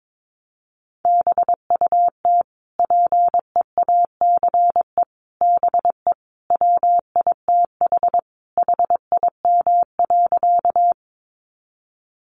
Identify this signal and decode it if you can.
Morse code, heard as a keyed tone, about 22 wpm, 700 Hz. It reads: BUT PEACE BE WITH HIM.
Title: Sentences from Top 1500 Words 22wpm